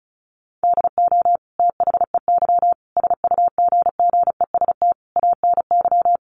Morse code, heard as a keyed tone, about 34 wpm, 700 Hz. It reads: DO THEY SUGGEST ANY
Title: Sentences from Top 1300 Words 35wpm